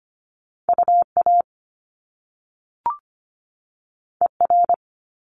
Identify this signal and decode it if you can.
Morse code, heard as a keyed tone, about 25 wpm, 700 Hz. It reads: UA ER
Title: Top Two Letter Combinations in Words 25wpm